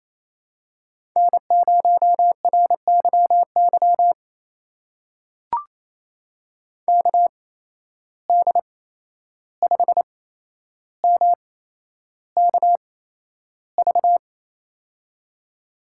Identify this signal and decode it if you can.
Morse code, heard as a keyed tone, about 27 wpm, spaced as for 11 wpm, 700 Hz. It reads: N0RYY KD5MKV
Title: USA Call Signs - 8x Character Spacing - 28wpm